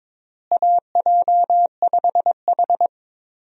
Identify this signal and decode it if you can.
Morse code, heard as a keyed tone, about 22 wpm, 700 Hz. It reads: AJ5H
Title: USA Call Signs - 4 Characters - Repeated 2x - 22wpm